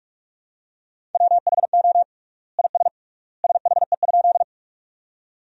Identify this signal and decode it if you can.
Morse code, heard as a keyed tone, about 46 wpm, 700 Hz. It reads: WHO IS SHE?
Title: 3x Word-Spacing - Sentences from Top 100 Words - 45wpm